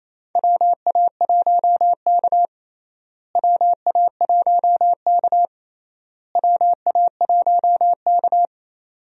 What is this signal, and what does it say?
Morse code, heard as a keyed tone, about 28 wpm, 700 Hz. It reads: WA1K WA1K WA1K
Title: USA Call Signs - 4 Characters - Repeated 3x - 28wpm